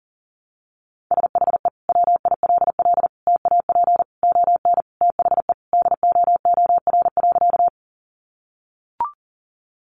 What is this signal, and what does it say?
Morse code, heard as a keyed tone, about 40 wpm, 700 Hz. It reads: SHE WILL TAP ON THE DOOR.
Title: Sentences from Top 1400 Words 40wpm